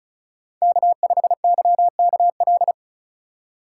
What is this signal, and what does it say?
Morse code, heard as a keyed tone, about 36 wpm, 700 Hz. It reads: K5YKL
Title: USA Call Signs - 5 Characters - 4x Character Spacing - 35wpm